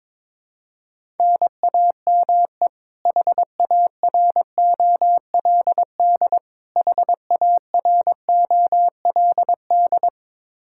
Morse code, heard as a keyed tone, about 21 wpm, 700 Hz. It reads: NAME HAROLD HAROLD
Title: K1USN SST QSO Practice 22wpm